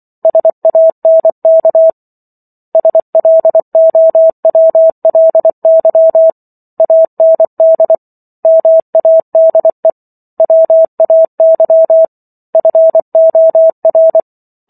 Morse code, heard as a keyed tone, about 24 wpm, 650 Hz. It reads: SANK SLOWLY AND MADE WAY FOR